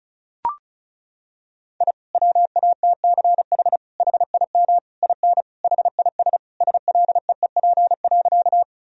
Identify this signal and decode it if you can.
Morse code, heard as a keyed tone, about 34 wpm, 700 Hz. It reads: I WATCH HIM IN HIS SLEEP.
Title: Sentences from Top 1000 Words - Review Part 1 - 35wpm